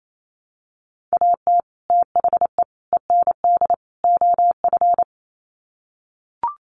Morse code, heard as a keyed tone, about 28 wpm, 700 Hz. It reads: AT THE END OF
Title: Sets of 4 Words 28wpm